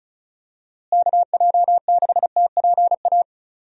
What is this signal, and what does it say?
Morse code, heard as a keyed tone, about 36 wpm, 700 Hz. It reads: KJ6TPA
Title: USA Call Signs - 6 Characters - Repeated 2x - 35wpm